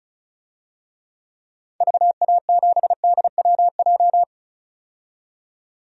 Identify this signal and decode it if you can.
Morse code, heard as a keyed tone, about 34 wpm, 700 Hz. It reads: VA7DWJ
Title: DXCC-100 Call Signs - 6 Characters - 35wpm